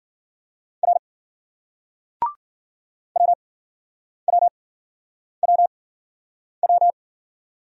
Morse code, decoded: R WWWW

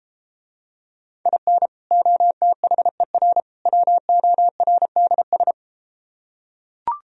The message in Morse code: IN OTHER WORDS

